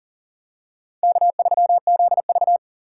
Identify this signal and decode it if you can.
Morse code, heard as a keyed tone, about 37 wpm, 700 Hz. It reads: K3ZV
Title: USA Call Signs - 4 Characters - 4x Character Spacing - 40wpm